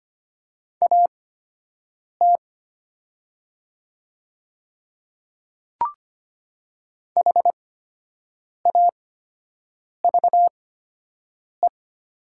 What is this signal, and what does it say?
Morse code, heard as a keyed tone, about 25 wpm, spaced as for 7 wpm, 700 Hz. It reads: AT HAVE
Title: 8x Character Spacing - Top 200 Words - 25wpm